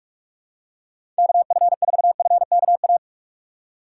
Morse code, heard as a keyed tone, about 42 wpm, 700 Hz. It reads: KF4FKA
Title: USA Call Signs - 6 Characters - 2x Character Spacing - 45wpm